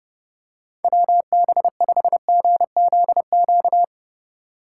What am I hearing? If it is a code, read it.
Morse code, heard as a keyed tone, about 30 wpm, 700 Hz. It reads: WB5GZQ
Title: USA Call Signs - 6 Characters - 4x Character Spacing - 30wpm